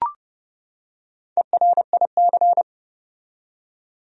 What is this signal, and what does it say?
Morse code, heard as a keyed tone, about 30 wpm, 700 Hz. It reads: ERIC
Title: USA Common Names - Max 4 Letters - 30wpm